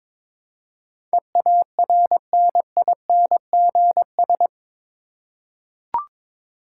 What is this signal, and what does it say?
Morse code, heard as a keyed tone, about 22 wpm, 700 Hz. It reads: EARNINGS
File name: practice / 22wpm / Eight Letter Words - 22wpm